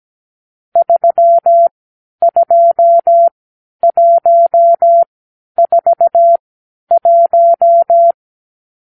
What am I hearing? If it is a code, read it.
Morse code, heard as a keyed tone, about 17 wpm, spaced as for 10 wpm, 680 Hz. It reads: 32141
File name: arrl / Review (1-5)